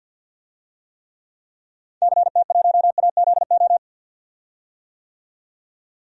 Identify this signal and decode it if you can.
Morse code, heard as a keyed tone, about 52 wpm, 700 Hz. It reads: CT1AZO
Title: DXCC-100 Call Signs - 6 Characters - 50wpm